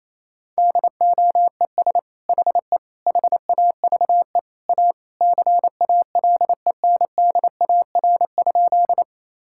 Morse code, heard as a keyed tone, about 28 wpm, 700 Hz. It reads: DOES HE HAVE A CALENDAR?
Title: Sentences from Top 1600 Words 28wpm